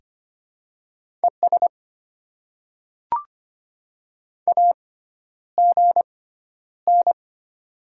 Morse code, heard as a keyed tone, about 25 wpm, spaced as for 11 wpm, 700 Hz. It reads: ES AGN
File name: Top 20 QSO Elements - 6x Character Spacing - 25wpm